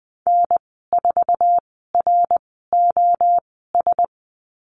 Morse code, heard as a keyed tone, about 19 wpm, spaced as for 14 wpm, 700 Hz. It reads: N4ROS
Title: 082 Lesson 14g - Number 2 - Call Signs 20wpm